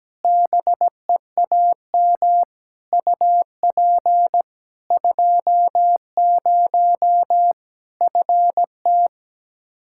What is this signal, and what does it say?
Morse code, heard as a keyed tone, about 17 wpm, 700 Hz. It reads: BEAM UP 20 FT
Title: Practice QSO - Exchange 2 - Info - 17wpm